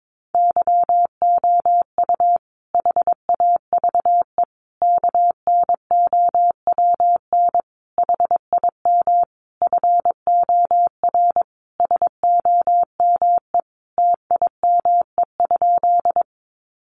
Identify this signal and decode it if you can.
Morse code, heard as a keyed tone, about 22 wpm, 700 Hz. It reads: YOU HAVE KNOWN HIM FOR SOME TIME?
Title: Sentences from Top 600 Words 22wpm